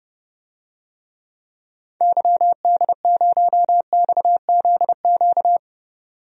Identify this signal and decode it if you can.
Morse code, heard as a keyed tone, about 32 wpm, 700 Hz. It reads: YD0XZQ
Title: DXCC-100 Call Signs - 6 Characters - 30wpm